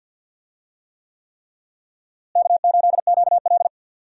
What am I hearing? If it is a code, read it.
Morse code, heard as a keyed tone, about 47 wpm, 700 Hz. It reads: K8QL
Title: USA Call Signs - 4 Characters - Repeated 3x - 50wpm